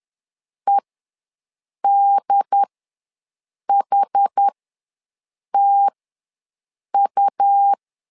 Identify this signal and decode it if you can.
Morse code, heard as a keyed tone, about 10 wpm, spaced as for 6 wpm, 800 Hz. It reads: EDHTU
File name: LEZIONI / 2 UHDN / 2 UHD ws 10wpm 2058